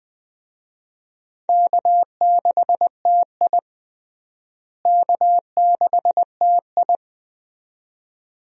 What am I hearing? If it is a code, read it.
Morse code, heard as a keyed tone, about 20 wpm, 700 Hz. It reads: K6TI K6TI
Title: USA Call Signs - 4 Characters - Repeated 2x - 20wpm